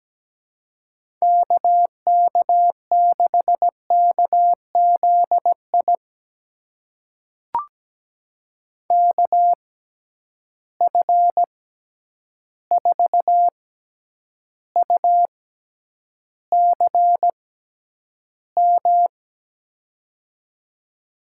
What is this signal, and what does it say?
Morse code, heard as a keyed tone, about 17 wpm, spaced as for 9 wpm, 700 Hz. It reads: KK6KZI KF4UCM
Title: USA Call Signs - 6x Character Spacing - 17wpm